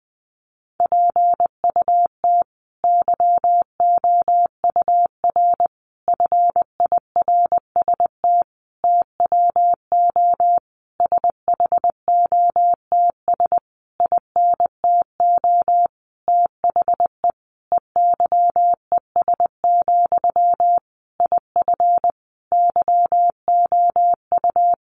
Morse code, heard as a keyed tone, about 20 wpm, 700 Hz. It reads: PUT YOUR FIRST TWO SHOTS INTO THE EYES, IF YOU
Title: A Sound Of Thunder 20wpm